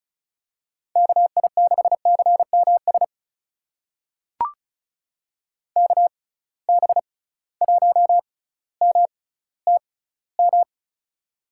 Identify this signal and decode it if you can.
Morse code, heard as a keyed tone, about 34 wpm, spaced as for 18 wpm, 700 Hz. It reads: KI6CMS KB1MTM
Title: USA Call Signs - 6 Characters - 6x Character Spacing - 35wpm